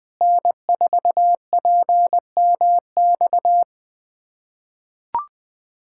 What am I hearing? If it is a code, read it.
Morse code, heard as a keyed tone, about 20 wpm, 700 Hz. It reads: N4PMX
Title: USA Callsign DE Callsign - 5 Characters - 20wpm